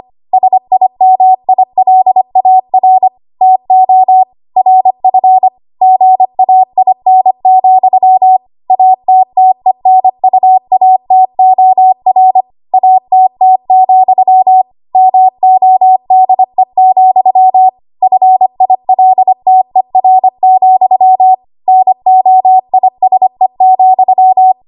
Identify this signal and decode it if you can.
Morse code, heard as a keyed tone, about 25 wpm, 750 Hz. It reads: SIMILAR TO RF GAIN, ATTENUATOR ATT, MODE, FILTER, NOISE,
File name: Archive / 25 WPM / 25WPM